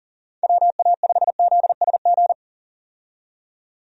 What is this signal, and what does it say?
Morse code, heard as a keyed tone, about 40 wpm, 700 Hz. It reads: WA5ZSG
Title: USA Call Signs - 6 Characters - 40wpm